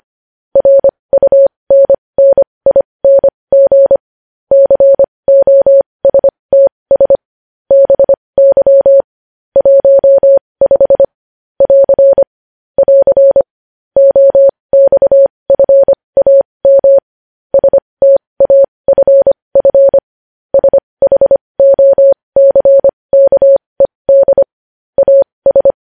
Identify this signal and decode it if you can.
Morse code, heard as a keyed tone, about 25 wpm, spaced as for 20 wpm, 550 Hz. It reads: RUNNING COSTS BY 15 <AR> <AR> OXFAM STAFF SHOCKED AS
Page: news-headlines-in-morse-code-at-20-wpm